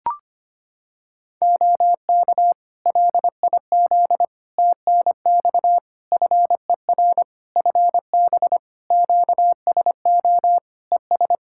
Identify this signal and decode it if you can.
Morse code, heard as a keyed tone, about 25 wpm, 700 Hz. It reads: OK LIZ TNX FER FB QSO ES
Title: K1USN SST QSO Practice 25wpm